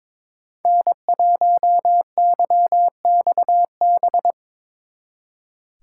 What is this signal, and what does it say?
Morse code, heard as a keyed tone, about 24 wpm, 700 Hz. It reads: N1YXB
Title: Speed Racing - USA 1x3 Call Signs 22wpm